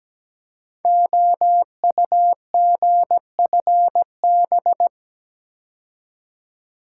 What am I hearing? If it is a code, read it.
Morse code, heard as a keyed tone, about 17 wpm, 700 Hz. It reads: OUGFB
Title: Code Groups - 5 Letters - 17wpm